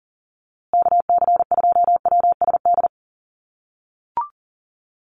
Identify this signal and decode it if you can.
Morse code, heard as a keyed tone, about 40 wpm, 700 Hz. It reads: KC2WSD
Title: USA Call Signs Repeated 3x 40wpm